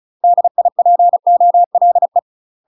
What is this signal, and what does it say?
Morse code, heard as a keyed tone, about 35 wpm, 700 Hz. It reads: DIPOLE
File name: UK MASTER QSOs TRAINING / 35WPM / UK MASTER QSOs 0038